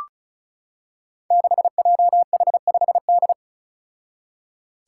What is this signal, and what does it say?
Morse code, heard as a keyed tone, about 36 wpm, 700 Hz. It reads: 6JH5D
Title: Five Challenging Letter-Number Combinations 35wpm